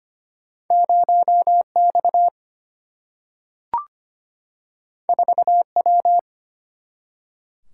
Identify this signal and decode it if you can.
Morse code, heard as a keyed tone, about 25 wpm, 700 Hz. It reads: 0 X 4 W
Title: Single-Letter Single-Number 25wpm